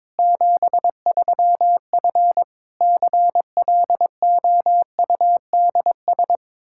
Morse code, heard as a keyed tone, about 22 wpm, 700 Hz. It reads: 73F CLOUDS
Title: Practice QSO - Exchange 2 - Info - 22wpm